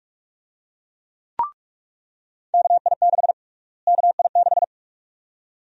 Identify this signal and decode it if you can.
Morse code, heard as a keyed tone, about 46 wpm, 690 Hz. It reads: KI6 KI6
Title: USA Call Sign Prefixes - Repeated 2x - 45wpm